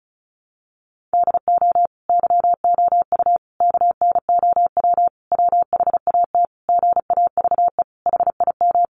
Morse code, heard as a keyed tone, about 34 wpm, 700 Hz. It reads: DO YOU KNOW WHAT GAVE HIM
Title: Sentences from Top 1300 Words 35wpm